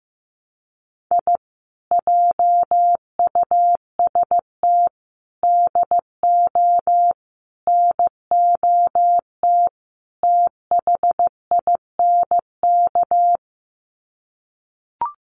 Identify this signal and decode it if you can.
Morse code, heard as a keyed tone, about 15 wpm, 700 Hz. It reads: I JUST DO NOT THINK
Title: Sets of 5 Words Encore 15wpm